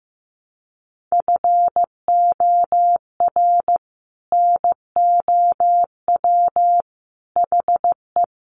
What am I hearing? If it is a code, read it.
Morse code, heard as a keyed tone, about 14 wpm, 700 Hz. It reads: FOR NOW HE
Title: Sentences from Top 1100 Words 15wpm